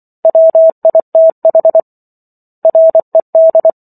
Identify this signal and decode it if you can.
Morse code, heard as a keyed tone, about 24 wpm, 650 Hz. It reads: WITH RED